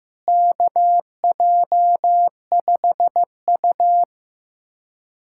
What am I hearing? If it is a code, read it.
Morse code, heard as a keyed tone, about 15 wpm, 700 Hz. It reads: KJ5U
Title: USA Callsign DE Callsign - 4 Characters - 15wpm